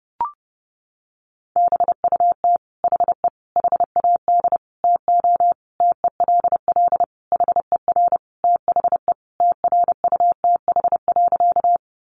Morse code, heard as a keyed tone, about 30 wpm, 700 Hz. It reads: BUT HE HAD TO TELL HER THE TRUTH.
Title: Sentences from Top 800 Words 30wpm